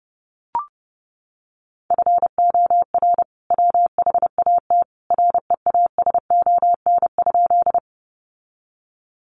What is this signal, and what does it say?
Morse code, heard as a keyed tone, about 30 wpm, 700 Hz. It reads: FOR WHAT REASON?
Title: Sentences from Top 700 Words 30wpm